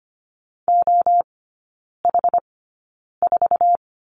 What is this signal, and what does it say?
Morse code, heard as a keyed tone, about 25 wpm, spaced as for 9 wpm, 700 Hz. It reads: OH4
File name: Lesson 14e - Number 2 - Three Characters 25wpm